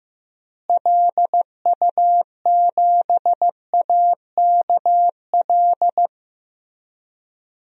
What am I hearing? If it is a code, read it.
Morse code, heard as a keyed tone, about 15 wpm, 700 Hz. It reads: LU7AKL
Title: DXCC-100 Call Signs - 6 Characters - Repeated 3x - 15wpm